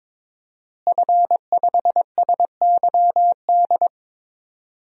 Morse code, heard as a keyed tone, about 22 wpm, 700 Hz. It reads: F5SYD
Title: DXCC-100 Call Signs - 5 Characters - Repeated 2x - 22wpm